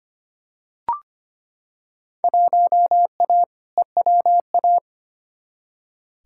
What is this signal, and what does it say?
Morse code, heard as a keyed tone, about 25 wpm, 700 Hz. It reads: 1A EWA
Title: Field Day Exchanges Encore - 25wpm